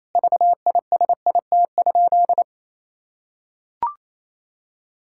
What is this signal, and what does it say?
Morse code, heard as a keyed tone, about 28 wpm, 700 Hz. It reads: VISIT?
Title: Sentences from Top 1600 Words 28wpm